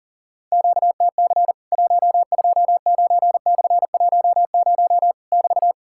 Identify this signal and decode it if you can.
Morse code, heard as a keyed tone, about 40 wpm, 700 Hz. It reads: QTC 129/10 <BT>
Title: Worked All Europe DX Contest - QTC Exchange - 40wpm